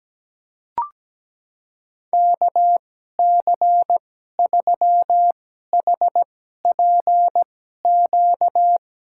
Morse code, heard as a keyed tone, about 17 wpm, spaced as for 12 wpm, 700 Hz. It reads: KC3HPQ
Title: USA Call Signs - 6 Characters - 2x Character Spacing - 17wpm